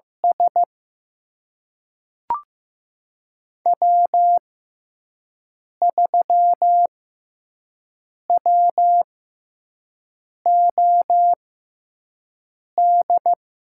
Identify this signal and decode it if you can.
Morse code, heard as a keyed tone, about 15 wpm, spaced as for 5 wpm, 700 Hz. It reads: S W3WOD
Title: USA Call Signs - 5 Characters - 6x Character Spacing - 15wpm